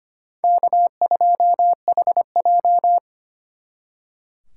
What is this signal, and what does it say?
Morse code, heard as a keyed tone, about 32 wpm, 700 Hz. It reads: K2HJ